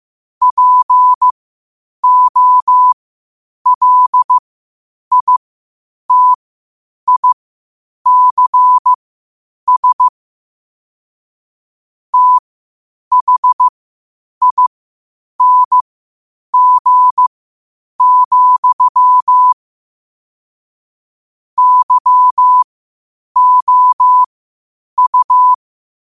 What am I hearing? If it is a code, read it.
Morse code, heard as a keyed tone, about 15 wpm, spaced as for 8 wpm, 1000 Hz. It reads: POLITICS THING, YOU
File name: qotdfiles / 8wpm